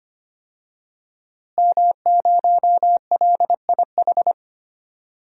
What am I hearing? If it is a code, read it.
Morse code, heard as a keyed tone, about 25 wpm, 700 Hz. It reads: M0LIH
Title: DXCC-100 Call Signs - 5 Characters - Repeated 2x - 25wpm